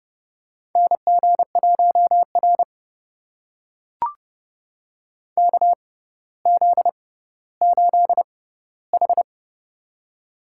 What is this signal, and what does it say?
Morse code, heard as a keyed tone, about 30 wpm, spaced as for 15 wpm, 700 Hz. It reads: NG1R KZ8H
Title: USA Call Signs - 4 Characters - 6x Character Spacing - 30wpm